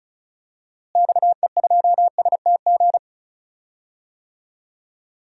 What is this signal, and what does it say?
Morse code, heard as a keyed tone, about 36 wpm, 700 Hz. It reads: XE2STG
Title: DXCC-100 Call Signs - 6 Characters - Repeated 2x - 35wpm